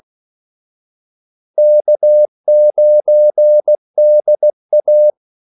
Morse code, heard as a keyed tone, about 16 wpm, 600 Hz. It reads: K9DA
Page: Kaldesignaler 16 wpm